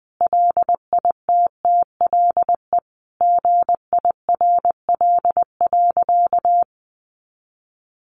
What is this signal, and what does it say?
Morse code, heard as a keyed tone, about 20 wpm, 700 Hz. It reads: LITTLE GIRL.
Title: Sentences from Top 400 Words 20wpm